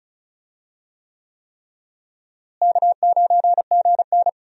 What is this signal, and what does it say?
Morse code, heard as a keyed tone, about 36 wpm, 700 Hz. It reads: K9GN
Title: USA Call Signs - 4 Characters - Repeated 3x - 35wpm